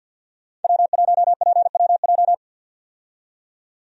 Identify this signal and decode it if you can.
Morse code, heard as a keyed tone, about 51 wpm, 700 Hz. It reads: W1PWJ
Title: USA Call Signs - 5 Characters - 50wpm